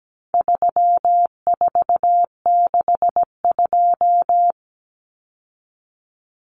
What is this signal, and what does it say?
Morse code, heard as a keyed tone, about 17 wpm, 700 Hz. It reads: 3462
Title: Common Cut Numbers - 4-Digit 17wpm